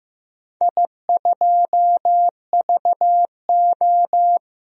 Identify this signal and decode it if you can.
Morse code, heard as a keyed tone, about 15 wpm, 700 Hz. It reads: I2VO
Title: DXCC-100 Call Signs - 4 Characters - Repeated 2x - 15wpm